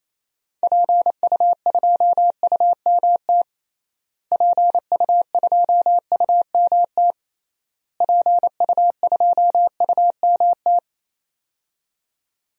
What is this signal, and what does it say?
Morse code, heard as a keyed tone, about 28 wpm, 700 Hz. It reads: PU2UMT PU2UMT PU2UMT
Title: DXCC-100 Call Signs - 6 Characters - Repeated 3x - 28wpm